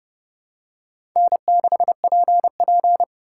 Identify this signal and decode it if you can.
Morse code, heard as a keyed tone, about 30 wpm, 700 Hz. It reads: N6PP
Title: USA Call Signs - 4 Characters - 3x Character Spacing - 30wpm